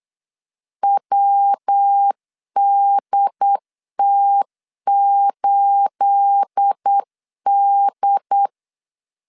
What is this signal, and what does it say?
Morse code, heard as a keyed tone, about 8 wpm, 800 Hz. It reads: WDT8D
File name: LEZIONI / 9 28 8wpm 3512